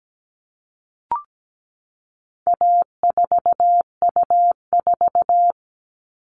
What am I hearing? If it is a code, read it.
Morse code, heard as a keyed tone, about 17 wpm, 700 Hz. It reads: A4U4
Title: All Cut Numbers - 4-Digit 17wpm